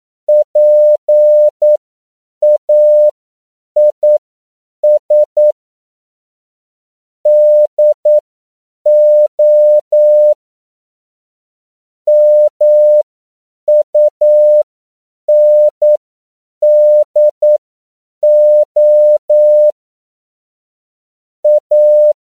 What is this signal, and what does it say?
Morse code, heard as a keyed tone, about 9 wpm, spaced as for 7 wpm, 600 Hz. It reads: PAIS DO MUNDO A
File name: CW / 10PPM